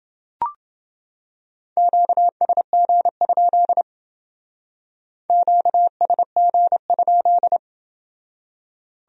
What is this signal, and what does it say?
Morse code, heard as a keyed tone, about 28 wpm, 700 Hz. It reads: QSG? QSG?